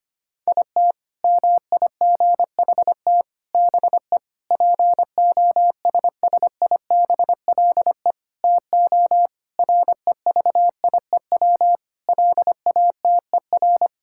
Morse code, heard as a keyed tone, about 25 wpm, 700 Hz. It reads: IT MIGHT BE POSSIBLE TO REVIEW LATER
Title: Sentences from Top 1000 Words - Review Part 6 - 25wpm